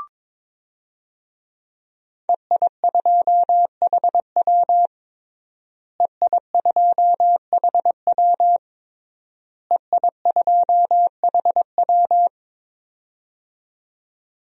Morse code, heard as a keyed tone, about 22 wpm, 700 Hz. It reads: EI2HW EI2HW EI2HW
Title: DXCC-100 Call Signs - 5 Characters - Repeated 3x - 22wpm